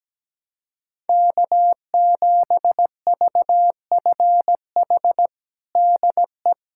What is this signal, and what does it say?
Morse code, heard as a keyed tone, about 17 wpm, 700 Hz. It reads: K7VFH DE
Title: USA Callsign DE Callsign - 5 Characters - 17wpm